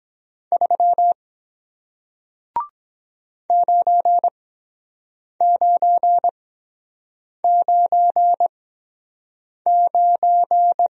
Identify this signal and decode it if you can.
Morse code, heard as a keyed tone, about 23 wpm, spaced as for 7 wpm, 700 Hz. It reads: 3 9999